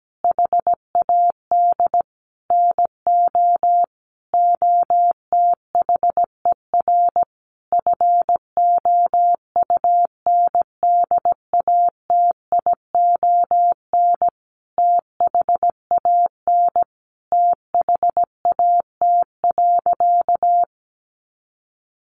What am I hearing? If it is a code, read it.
Morse code, heard as a keyed tone, about 17 wpm, 700 Hz. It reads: HAD NO OTHER FOUNDATION THAN THAT.
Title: Sentences from Top 1200 Words 17wpm